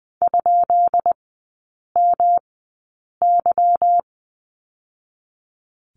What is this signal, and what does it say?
Morse code, heard as a keyed tone, about 20 wpm, spaced as for 8 wpm, 690 Hz. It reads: ?MY
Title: 198 Lesson 32e - Letter k - Three Characters 20wpm